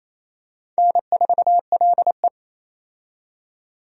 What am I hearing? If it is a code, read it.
Morse code, heard as a keyed tone, about 27 wpm, 700 Hz. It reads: N4LE
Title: USA Call Signs - 4 Characters - 28wpm